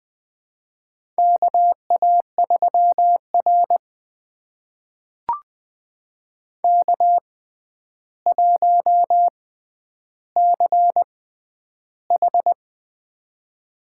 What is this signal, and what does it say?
Morse code, heard as a keyed tone, about 20 wpm, spaced as for 10 wpm, 700 Hz. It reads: KA3R K1CH
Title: USA Call Signs - 6x Character Spacing - 20wpm